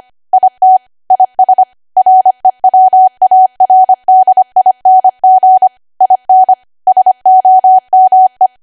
Morse code, heard as a keyed tone, about 25 wpm, 750 Hz. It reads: IT IS REWARDING IN SOME
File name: W1AW-Level3/140218 25WPM